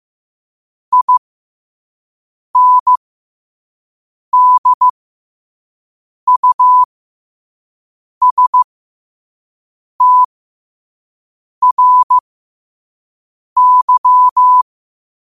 Morse code, heard as a keyed tone, about 15 wpm, spaced as for 5 wpm, 1000 Hz. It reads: INDUSTRY
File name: qotdfiles / 5wpm